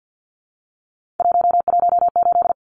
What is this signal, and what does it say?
Morse code, heard as a keyed tone, about 50 wpm, 700 Hz. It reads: 118
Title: Numbers - 3-Digit 50wpm